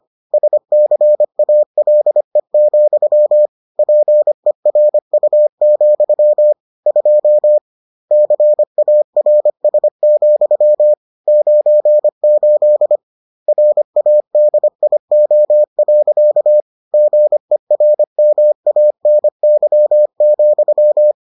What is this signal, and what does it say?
Morse code, heard as a keyed tone, about 25 wpm, 600 Hz. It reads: SCALE, PERU, 2 CARS, 98 RADIO. GERMANY,